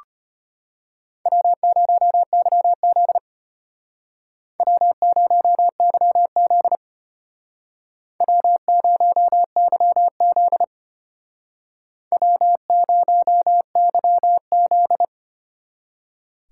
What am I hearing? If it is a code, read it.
Morse code, heard as a keyed tone, about 33 wpm, 700 Hz. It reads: W0YZ W0YZ W0YZ W0YZ